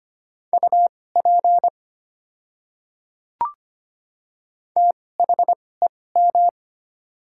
Morse code, heard as a keyed tone, about 25 wpm, spaced as for 18 wpm, 700 Hz. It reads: UP THEM
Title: 2x Character Spacing - Top 200 Words - 25wpm